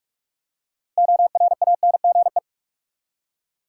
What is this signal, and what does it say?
Morse code, heard as a keyed tone, about 44 wpm, 700 Hz. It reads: ORANGE
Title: Six Letter Words - 45wpm